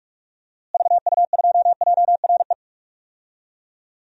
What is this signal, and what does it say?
Morse code, heard as a keyed tone, about 46 wpm, 700 Hz. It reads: VU2JRE
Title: DXCC-100 Call Signs - 6 Characters - Repeated 3x - 45wpm